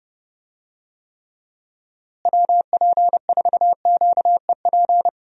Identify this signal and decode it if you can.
Morse code, heard as a keyed tone, about 30 wpm, 700 Hz. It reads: WP4QEP
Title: USA Call Signs - 6 Characters - Repeated 3x - 30wpm